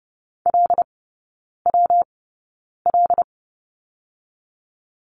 Morse code, heard as a keyed tone, about 30 wpm, 700 Hz. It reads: L W L
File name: Lesson 19e - Letter w - Three Characters 30wpm